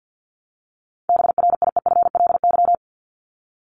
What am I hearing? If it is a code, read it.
Morse code, heard as a keyed tone, about 48 wpm, 700 Hz. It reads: BRIEFLY
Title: LY Suffix - Top 100 Words with Base Words 50wpm